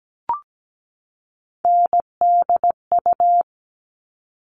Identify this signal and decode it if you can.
Morse code, heard as a keyed tone, about 17 wpm, 700 Hz. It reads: NDU
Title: All Cut Numbers - 3-Digit 17wpm